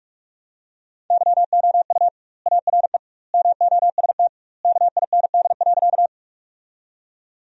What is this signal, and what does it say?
Morse code, heard as a keyed tone, about 46 wpm, 700 Hz. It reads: YOU ARE MOST KIND.
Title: 2x Word-Spacing - Sentences from Top 200 Words - 45wpm